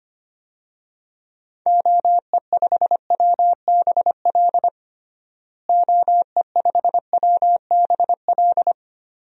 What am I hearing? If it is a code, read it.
Morse code, heard as a keyed tone, about 25 wpm, 700 Hz. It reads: OE5WBL OE5WBL
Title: DXCC-100 Call Signs - 6 Characters - Repeated 2x - 25wpm